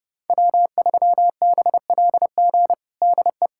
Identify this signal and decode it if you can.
Morse code, heard as a keyed tone, about 32 wpm, 700 Hz. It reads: W3BLG DE
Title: USA Callsign DE Callsign - 5 Characters - 30wpm